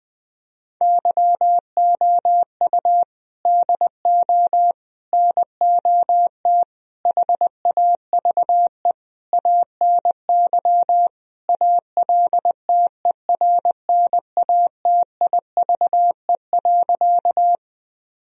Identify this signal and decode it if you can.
Morse code, heard as a keyed tone, about 21 wpm, 700 Hz. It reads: YOU DO NOT HAVE ANY ALTERNATIVE.
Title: Sentences from Top 1000 Words - Review Part 4 - 20wpm